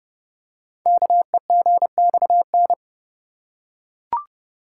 Morse code, heard as a keyed tone, about 30 wpm, 700 Hz. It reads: KEGXN
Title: Code Groups - 5 Letters - 30wpm